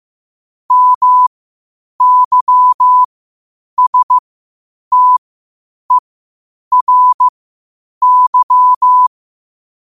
Morse code, heard as a keyed tone, about 15 wpm, spaced as for 8 wpm, 1000 Hz. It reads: MYSTERY
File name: qotdfiles / 8wpm